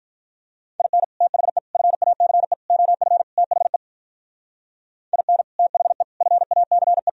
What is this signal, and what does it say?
Morse code, heard as a keyed tone, about 43 wpm, 700 Hz. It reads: IN THE FACE OF THE IN THE FACE
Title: Speed-Racing - Sets of 5 Words 35wpm